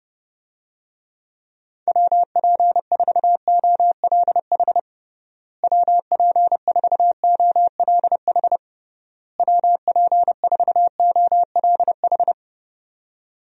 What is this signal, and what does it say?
Morse code, heard as a keyed tone, about 32 wpm, 700 Hz. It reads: WP4OLH WP4OLH WP4OLH
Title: USA Call Signs - 6 Characters - Repeated 3x - 30wpm